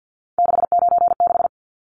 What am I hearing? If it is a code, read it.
Morse code, heard as a keyed tone, about 50 wpm, 700 Hz. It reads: 696
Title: Numbers - 3-Digit 50wpm